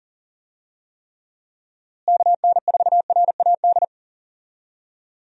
Morse code, heard as a keyed tone, about 40 wpm, 700 Hz. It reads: KN4RAD
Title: USA Call Signs - 6 Characters - Repeated 2x - 40wpm